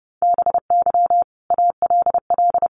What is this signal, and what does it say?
Morse code, heard as a keyed tone, about 30 wpm, 700 Hz. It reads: BY ALL
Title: Sentences from Top 1300 Words 30wpm